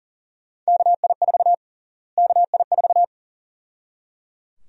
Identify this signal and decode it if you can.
Morse code, heard as a keyed tone, about 37 wpm, 700 Hz. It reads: KI4 KI4
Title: USA Call Sign Prefixes - Repeated 2x - 40wpm